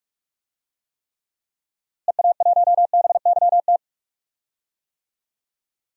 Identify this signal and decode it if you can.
Morse code, heard as a keyed tone, about 41 wpm, 700 Hz. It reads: EA1BYT
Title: DXCC-100 Call Signs - 6 Characters - Repeated 3x - 45wpm